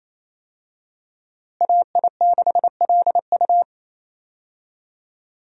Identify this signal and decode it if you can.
Morse code, heard as a keyed tone, about 28 wpm, 700 Hz. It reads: AI6LU
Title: USA Call Signs - 5 Characters - Repeated 3x - 28wpm